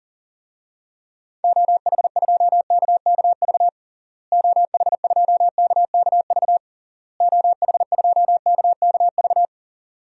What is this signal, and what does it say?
Morse code, heard as a keyed tone, about 37 wpm, 700 Hz. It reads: OH2KKV OH2KKV OH2KKV
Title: DXCC-100 Call Signs - 6 Characters - Repeated 3x - 40wpm